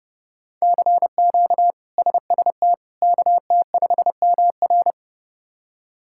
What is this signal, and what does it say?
Morse code, heard as a keyed tone, about 30 wpm, 700 Hz. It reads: CQ SST KT5MR
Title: K1USN SST QSO Practice - 30wpm